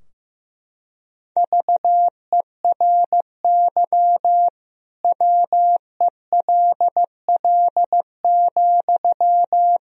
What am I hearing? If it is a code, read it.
Morse code, heard as a keyed tone, about 16 wpm, 700 Hz. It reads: VERY WELL,
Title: Sentences from Top 1000 Words - Review Part 4 - 15wpm